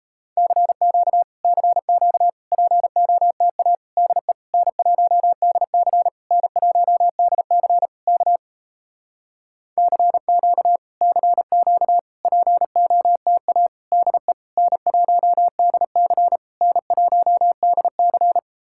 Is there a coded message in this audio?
Yes: Morse code from 0.4 to 18.4 s, heard as a keyed tone, about 38 wpm, 700 Hz. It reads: CQ CQ POTA DE N1DC N1DC K CQ CQ POTA DE N1DC N1DC